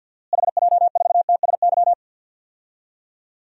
Morse code, heard as a keyed tone, about 47 wpm, 700 Hz. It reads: HJ4TSY
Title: DXCC-100 Call Signs - 6 Characters - Repeated 3x - 50wpm